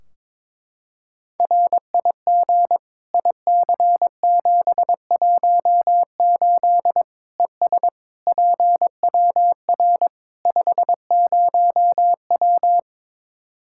Morse code, heard as a keyed tone, about 22 wpm, 700 Hz. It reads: RIG IC718 ES PWR 50W
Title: Practice QSO - Exchange 2 - Info - 22wpm